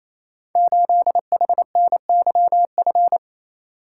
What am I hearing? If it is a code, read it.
Morse code, heard as a keyed tone, about 27 wpm, 700 Hz. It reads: 8HNYF
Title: Code Groups - 5 Letter:Numbers - 28wpm